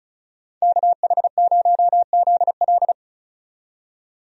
Morse code, heard as a keyed tone, about 36 wpm, 700 Hz. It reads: KH0ZL
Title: USA Call Signs - 5 Characters - 35wpm